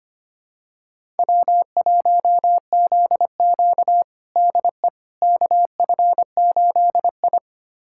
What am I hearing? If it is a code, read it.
Morse code, heard as a keyed tone, about 25 wpm, 700 Hz. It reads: W1ZQ DE KF8I
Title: USA Callsign DE Callsign - 4 Characters - 25wpm